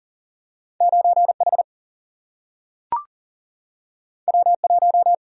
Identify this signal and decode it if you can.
Morse code, heard as a keyed tone, about 40 wpm, 700 Hz. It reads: 9H W1
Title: Single-Letter Single-Number 40wpm